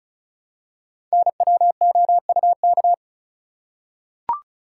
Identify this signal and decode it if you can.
Morse code, heard as a keyed tone, about 36 wpm, 700 Hz. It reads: NWOUK
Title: Code Groups - 5 Letters - 35wpm